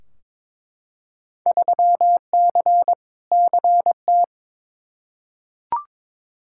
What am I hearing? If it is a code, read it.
Morse code, heard as a keyed tone, about 21 wpm, 700 Hz. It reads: 3C CT
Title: Field Day Exchanges Encore - 22wpm